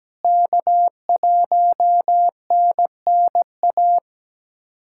Morse code, heard as a keyed tone, about 17 wpm, 700 Hz. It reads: K1NNA
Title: USA Callsign DE Callsign 17wpm